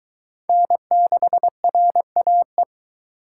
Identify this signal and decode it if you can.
Morse code, heard as a keyed tone, about 22 wpm, 700 Hz. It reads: N6RAE